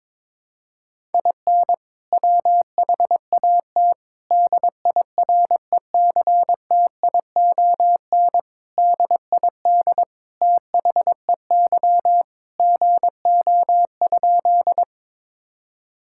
Morse code, heard as a keyed tone, about 22 wpm, 700 Hz. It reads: IN WHAT DIRECTION DID THEY GO?
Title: Sentences from Top 1000 Words - Review Part 5 - 22wpm